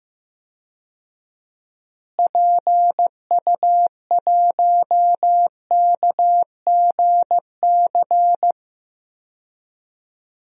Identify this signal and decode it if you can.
Morse code, heard as a keyed tone, about 15 wpm, 700 Hz. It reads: PU1KGC
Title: DXCC-100 Call Signs - 6 Characters - Repeated 3x - 15wpm